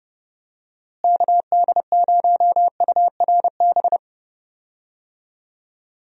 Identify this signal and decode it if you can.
Morse code, heard as a keyed tone, about 31 wpm, 700 Hz. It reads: KD0URB
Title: USA Call Signs - 6 Characters - Repeated 3x - 30wpm